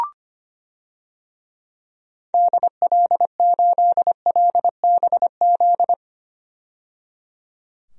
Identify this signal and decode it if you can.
Morse code, heard as a keyed tone, about 25 wpm, 700 Hz. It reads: DL8LBZ
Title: DXCC-100 Call Signs - 6 Characters - 25wpm